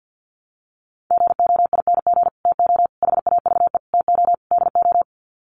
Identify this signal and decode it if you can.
Morse code, heard as a keyed tone, about 48 wpm, 700 Hz. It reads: GOING TO HAVE TO DO
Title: Sets of 5 Words Encore 50wpm